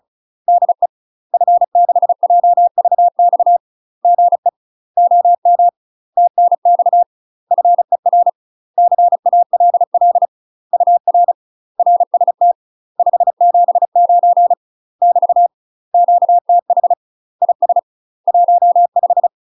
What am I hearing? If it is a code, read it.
Morse code, heard as a keyed tone, about 34 wpm, 700 Hz. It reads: DE F6JVX GE OM TNX FER CALL UR RST 579 <BT> QTH IS 15
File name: UK MASTER QSOs TRAINING / 35WPM / UK MASTER QSOs 0021